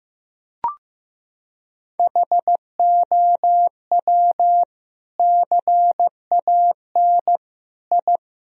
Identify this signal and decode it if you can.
Morse code, heard as a keyed tone, about 15 wpm, 700 Hz. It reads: HOW CAN I
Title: Sentences from Top 1000 Words - Review Part 2 - 15wpm